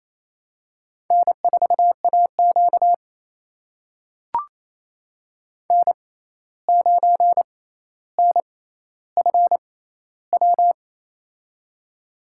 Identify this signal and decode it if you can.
Morse code, heard as a keyed tone, about 28 wpm, spaced as for 13 wpm, 700 Hz. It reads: N4AQ N9NFW
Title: USA Call Signs - 6x Character Spacing - 28wpm